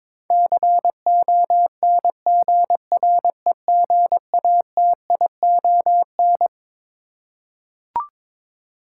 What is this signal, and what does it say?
Morse code, heard as a keyed tone, about 22 wpm, 700 Hz. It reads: CONGREGATION
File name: Twelve Letter Words - 22wpm